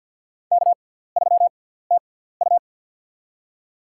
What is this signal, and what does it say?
Morse code, heard as a keyed tone, about 50 wpm, spaced as for 17 wpm, 700 Hz. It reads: K3TU
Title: USA Call Signs - 4 Characters - 6x Character Spacing - 50wpm